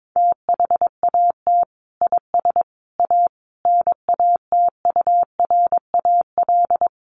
Morse code, heard as a keyed tone, about 22 wpm, 700 Hz. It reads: THAT IS A NATURAL
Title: Sentences from Top 1200 Words 22wpm